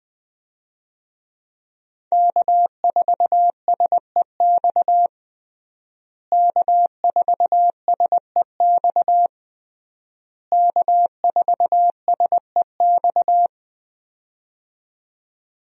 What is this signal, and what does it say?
Morse code, heard as a keyed tone, about 20 wpm, 700 Hz. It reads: K4SEX K4SEX K4SEX
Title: USA Call Signs - 5 Characters - Repeated 3x - 20wpm